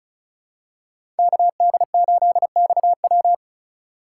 Morse code, heard as a keyed tone, about 36 wpm, 700 Hz. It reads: KD8XW
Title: USA Call Signs - 5 Characters - 4x Character Spacing - 35wpm